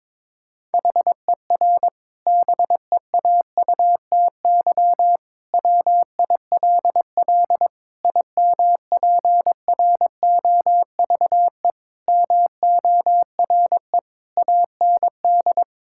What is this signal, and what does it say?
Morse code, heard as a keyed tone, about 21 wpm, 700 Hz. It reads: HER BEAUTY WILL IMPROVE MORE AND
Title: Sentences from Top 1000 Words - Review Part 5 - 22wpm